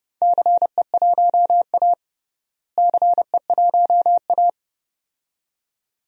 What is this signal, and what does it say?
Morse code, heard as a keyed tone, about 30 wpm, 700 Hz. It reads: CE1A CE1A
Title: DXCC-100 Call Signs - 4 Characters - Repeated 2x - 30wpm